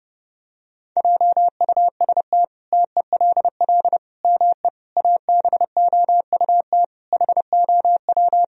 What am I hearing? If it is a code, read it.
Morse code, heard as a keyed tone, about 30 wpm, 700 Hz. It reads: JUST TELL ME ABOUT HOW
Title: Sentences from Top 1000 Words - Review Part 1 - 30wpm